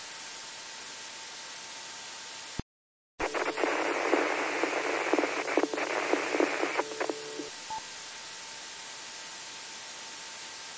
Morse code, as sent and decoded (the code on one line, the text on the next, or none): none